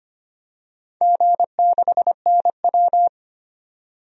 Morse code, heard as a keyed tone, about 25 wpm, 700 Hz. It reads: G6NW
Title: DXCC-100 Call Signs - 4 Characters - 25wpm